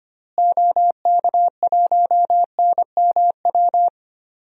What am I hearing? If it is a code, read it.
Morse code, heard as a keyed tone, about 25 wpm, 690 Hz. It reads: OK1NMW
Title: DXCC-100 Call Signs - 6 Characters - Repeated 3x - 25wpm